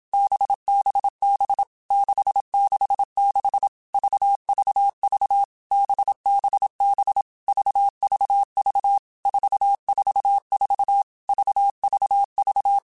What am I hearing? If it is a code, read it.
Morse code, heard as a keyed tone, about 27 wpm, 800 Hz. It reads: BBB 666 VVV BBB VVV 444 VVV
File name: Basic Letter Groups By7Groups / VB46 / VB46@25